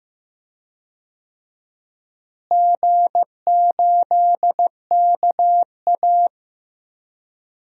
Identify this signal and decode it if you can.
Morse code, heard as a keyed tone, about 15 wpm, 700 Hz. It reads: G8KA
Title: DXCC-100 Call Signs - 4 Characters - Repeated 2x - 15wpm